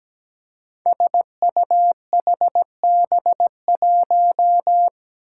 Morse code, heard as a keyed tone, about 17 wpm, 700 Hz. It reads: SUHB1
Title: Five Challenging Letter-Number Combinations 17wpm